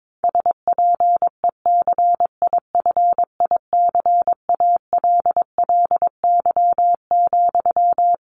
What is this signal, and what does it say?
Morse code, heard as a keyed tone, about 22 wpm, 700 Hz. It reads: SPECIFICALLY,
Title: Sentences from Top 1100 Words 22wpm